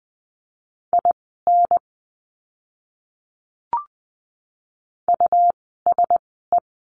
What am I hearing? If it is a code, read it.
Morse code, heard as a keyed tone, about 20 wpm, spaced as for 15 wpm, 700 Hz. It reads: IN USE
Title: 237 Lesson 38f - Letter q - Words 20wpm